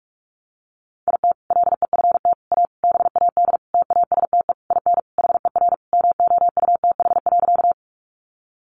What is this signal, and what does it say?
Morse code, heard as a keyed tone, about 44 wpm, 700 Hz. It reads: IT LEFT A BAD TASTE IN HER MOUTH.
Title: Sentences from Top 1500 Words 45wpm